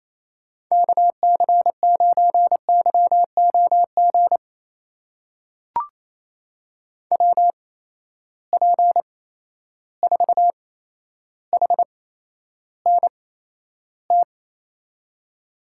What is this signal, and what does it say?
Morse code, heard as a keyed tone, about 28 wpm, spaced as for 12 wpm, 700 Hz. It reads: KC9YOG WP4HNT
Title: USA Call Signs - 6 Characters - 8x Character Spacing - 28wpm